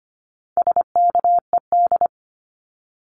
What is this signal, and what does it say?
Morse code, heard as a keyed tone, about 25 wpm, 700 Hz. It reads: SKED
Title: Abbreviations Repeated 3x - 25wpm